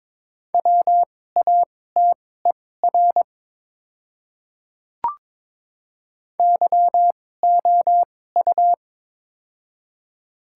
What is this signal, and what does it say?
Morse code, heard as a keyed tone, about 22 wpm, spaced as for 16 wpm, 700 Hz. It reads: WATER YOU
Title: 2x Character Spacing - Top 200 Words - 22wpm